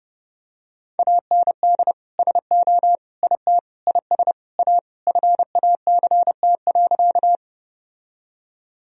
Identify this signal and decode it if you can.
Morse code, heard as a keyed tone, about 32 wpm, 700 Hz. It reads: AND SO IT IS A FACT.
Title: Sentences from Top 1000 Words - Review Part 3 - 30wpm